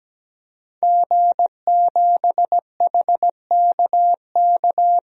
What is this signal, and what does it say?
Morse code, heard as a keyed tone, about 17 wpm, 700 Hz. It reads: G7HKK
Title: DXCC-100 Call Signs - 5 Characters - 17wpm